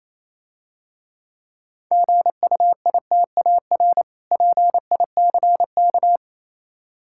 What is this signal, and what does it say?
Morse code, heard as a keyed tone, about 27 wpm, 700 Hz. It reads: GUITAR PICK
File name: Easy Words - Speed-Racing with Hints - 22wpm